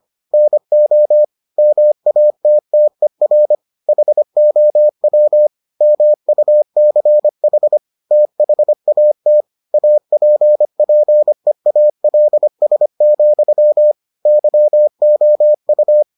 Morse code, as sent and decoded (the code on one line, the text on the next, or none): -. --- / -- .- - - . .-. / .... --- .-- / -- ..- -.-. .... / - .... .- - / .- .--. .--. .- .-.. ... --..-- / -.-- --- ..-
NO MATTER HOW MUCH THAT APPEALS, YOU